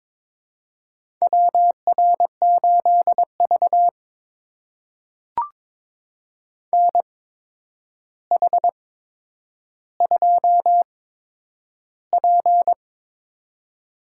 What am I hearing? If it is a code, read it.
Morse code, heard as a keyed tone, about 22 wpm, spaced as for 9 wpm, 700 Hz. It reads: WR8V NH2P
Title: USA Call Signs - 4 Characters - 8x Character Spacing - 22wpm